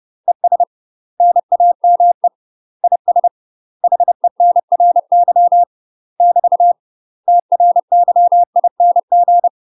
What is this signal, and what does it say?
Morse code, heard as a keyed tone, about 30 wpm, 700 Hz. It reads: ES NAME IS HENRY <BT> TRYING
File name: UK MASTER QSOs TRAINING / 30WPM / UK MASTER QSOs 0007